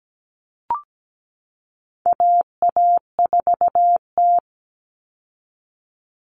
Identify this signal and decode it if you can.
Morse code, heard as a keyed tone, about 17 wpm, 700 Hz. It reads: AA4T
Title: All Cut Numbers - 4-Digit 17wpm